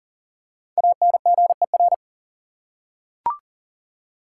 Morse code, heard as a keyed tone, about 40 wpm, 700 Hz. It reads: ANGER
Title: Five Letter Words - 40wpm